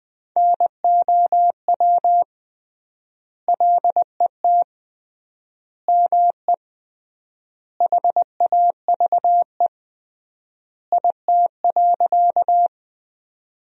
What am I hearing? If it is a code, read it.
Morse code, heard as a keyed tone, about 20 wpm, 700 Hz. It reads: NOW LET ME HAVE IT.
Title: 3x Word-Spacing - Sentences from Top 300 Words - 20wpm